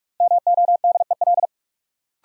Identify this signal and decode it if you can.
Morse code, heard as a keyed tone, about 42 wpm, 700 Hz. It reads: MODEL
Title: Sentences from Top 1000 Words - New Word Drill - 45wpm